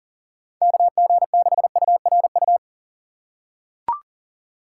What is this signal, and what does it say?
Morse code, heard as a keyed tone, about 40 wpm, 700 Hz. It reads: KG6URU
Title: USA Call Signs - 6 Characters - 4x Character Spacing - 40wpm